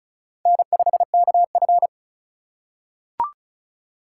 Morse code, heard as a keyed tone, about 36 wpm, 700 Hz. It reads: N5KF
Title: USA Call Signs - 4 Characters - 3x Character Spacing - 35wpm